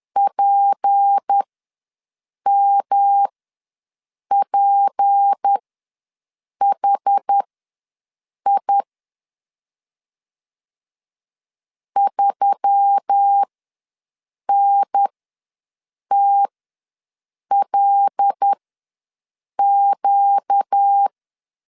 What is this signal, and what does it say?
Morse code, heard as a keyed tone, about 10 wpm, spaced as for 6 wpm, 800 Hz. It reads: PMPHI 3NTLQ
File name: LEZIONI / 6 PKX0 / 6 P ws 10wpm 3751